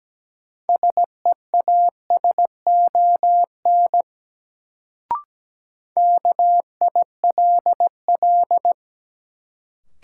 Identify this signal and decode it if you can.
Morse code, heard as a keyed tone, about 17 wpm, 700 Hz. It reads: SEASON KILL
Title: Sentences from Top 1400 Words - New Word Drill - 17wpm